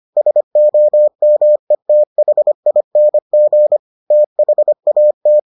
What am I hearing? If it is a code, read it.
Morse code, heard as a keyed tone, about 25 wpm, 600 Hz. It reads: SOMETHING THAT